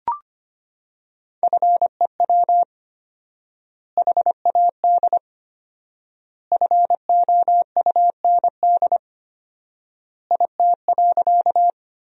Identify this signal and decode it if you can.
Morse code, heard as a keyed tone, about 25 wpm, 700 Hz. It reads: FEW HAD FOUND IT.
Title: 4x Word-Spacing - Sentences from Top 300 Words - 25wpm